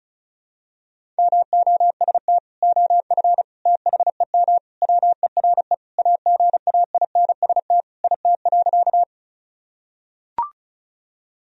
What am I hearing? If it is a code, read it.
Morse code, heard as a keyed tone, about 34 wpm, 690 Hz. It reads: MOST OF THEM WERE AGAINST IT.
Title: Sentences from Top 1000 Words - Review Part 1 - 35wpm